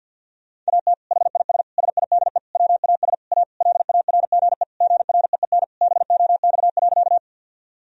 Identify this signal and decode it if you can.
Morse code, heard as a keyed tone, about 47 wpm, 700 Hz. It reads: AT HIS SIDE WAS A LARGE GREEN BOX.
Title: Sentences from Top 1000 Words - Review Part 2 - 50wpm